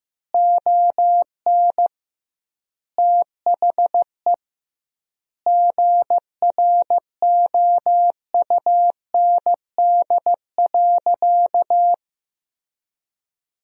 Morse code, heard as a keyed tone, about 15 wpm, 700 Hz. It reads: ON THE GROUND.
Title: 2x Word-Spacing - Sentences from Top 500 Words - 15wpm